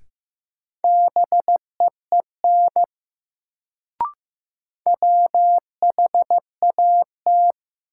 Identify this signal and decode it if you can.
Morse code, heard as a keyed tone, about 15 wpm, 700 Hz. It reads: BEEN WHAT
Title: Sentences from Top 100 Words - New Word Drill - 15wpm